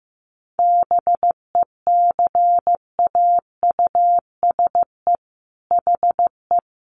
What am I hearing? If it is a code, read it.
Morse code, heard as a keyed tone, about 15 wpm, 700 Hz. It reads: BECAUSE HE
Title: Sentences from Top 1400 Words 15wpm